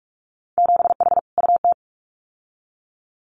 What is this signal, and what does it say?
Morse code, heard as a keyed tone, about 46 wpm, 700 Hz. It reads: 7H UT
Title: Winter Field Day Exchanges Encore 45wpm